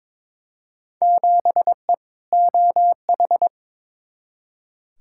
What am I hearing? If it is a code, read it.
Morse code, heard as a keyed tone, about 22 wpm, 700 Hz. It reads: 7E OH
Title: Field Day Exchanges Encore - 22wpm